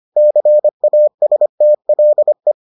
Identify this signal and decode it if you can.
Morse code, heard as a keyed tone, about 25 wpm, 600 Hz. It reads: CASTLE